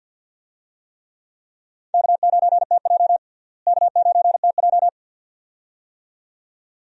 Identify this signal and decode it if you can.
Morse code, heard as a keyed tone, about 51 wpm, 700 Hz. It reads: K9TJ K9TJ
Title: USA Call Signs - 4 Characters - Repeated 2x - 50wpm